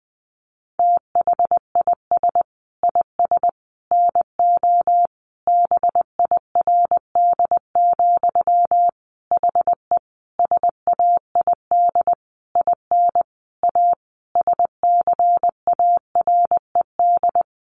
Morse code, heard as a keyed tone, about 19 wpm, 700 Hz. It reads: THIS IS NO BIRD, HE SAID IN A SCARED
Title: Sentences from Top 700 Words 20wpm